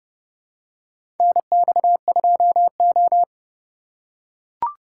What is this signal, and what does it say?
Morse code, heard as a keyed tone, about 30 wpm, 700 Hz. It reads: NX2O
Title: USA Call Signs - 4 Characters - 4x Character Spacing - 30wpm